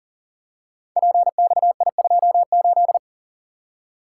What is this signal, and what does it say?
Morse code, heard as a keyed tone, about 39 wpm, 700 Hz. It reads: PXI28
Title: Code Groups - 5 Letter:Numbers - 40wpm